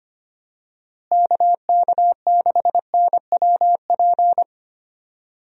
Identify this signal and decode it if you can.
Morse code, heard as a keyed tone, about 25 wpm, 700 Hz. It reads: KK6NWP
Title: USA Call Signs - 6 Characters - Repeated 2x - 25wpm